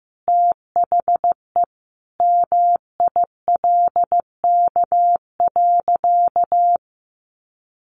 Morse code, heard as a keyed tone, about 15 wpm, 700 Hz. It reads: THE MILK.
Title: Additional Easy Phrases 15wpm